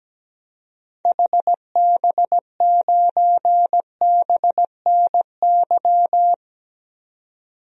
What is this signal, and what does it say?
Morse code, heard as a keyed tone, about 17 wpm, 700 Hz. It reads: HB9BNY
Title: DXCC-100 Call Signs - 6 Characters - 17wpm